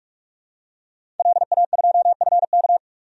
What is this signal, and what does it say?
Morse code, heard as a keyed tone, about 46 wpm, 700 Hz. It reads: LA2FK
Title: DXCC-100 Call Signs - 5 Characters - 45wpm